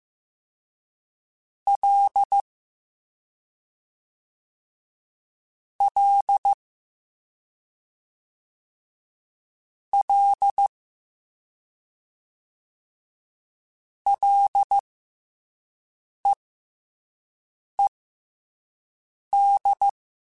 Morse code, heard as a keyed tone, about 15 wpm, spaced as for 5 wpm, 800 Hz. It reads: L L L LEED
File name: F and L / F and L mp3 / F and L Prefixes 3x2 Twice @5